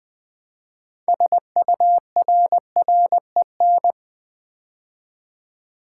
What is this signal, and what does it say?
Morse code, heard as a keyed tone, about 20 wpm, 700 Hz. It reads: SURREN
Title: Build A Word - 3 Letters At A Time - 9-Letter Words - 20wpm